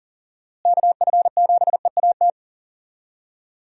Morse code, heard as a keyed tone, about 40 wpm, 700 Hz. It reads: KF7EAT
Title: USA Call Signs - 6 Characters - Repeated 3x - 40wpm